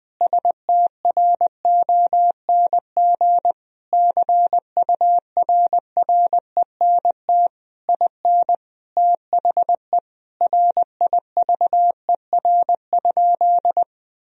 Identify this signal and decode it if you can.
Morse code, heard as a keyed tone, about 20 wpm, 700 Hz. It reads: STRONG CURRENT IN THE RIVER?
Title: Sentences from Top 1000 Words - Review Part 4 - 20wpm